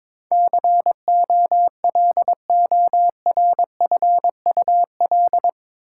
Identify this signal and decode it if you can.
Morse code, heard as a keyed tone, about 22 wpm, 700 Hz. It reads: COLORFUL
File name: practice / 22wpm / Eight Letter Words - 22wpm